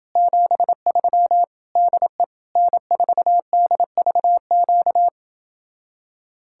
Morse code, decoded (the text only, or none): 73 DE N4DVQ